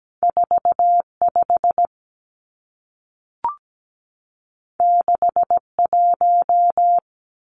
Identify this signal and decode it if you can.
Morse code, heard as a keyed tone, about 17 wpm, 700 Hz. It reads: 45 61
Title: Numbers - 2-Digit 17wpm